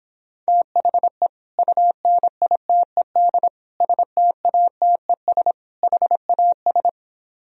Morse code, heard as a keyed tone, about 26 wpm, 700 Hz. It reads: THE UNITED STATES HAS